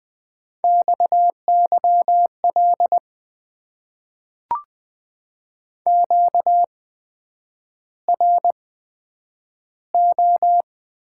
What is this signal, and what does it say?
Morse code, heard as a keyed tone, about 20 wpm, spaced as for 9 wpm, 700 Hz. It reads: XYL QRO
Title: Top 90 QSO Elements - 8x Character Spacing - 20wpm